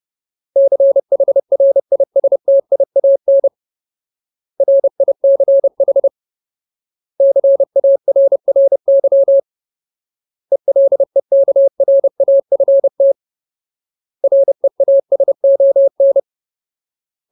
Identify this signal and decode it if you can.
Morse code, heard as a keyed tone, about 30 wpm, 550 Hz. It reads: CHRISTIAN RICH CARRY ELEKRAFT REASON
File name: Ultimate Word List / 30WPM Ultimate